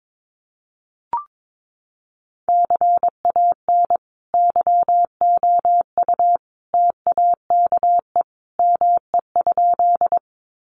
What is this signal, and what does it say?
Morse code, heard as a keyed tone, about 22 wpm, 700 Hz. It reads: CAN YOU TAKE ME?
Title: Sentences from Top 200 Words 22wpm